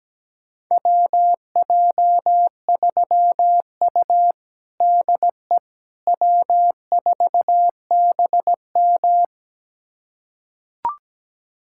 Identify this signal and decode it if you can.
Morse code, heard as a keyed tone, about 17 wpm, 700 Hz. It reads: WJ3U DE W4BM
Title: USA Callsign DE Callsign - 4 Characters - 17wpm